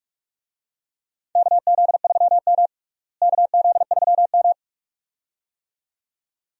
Morse code, heard as a keyed tone, about 46 wpm, 700 Hz. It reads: KZ3M KZ3M
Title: USA Call Signs - 4 Characters - Repeated 2x - 45wpm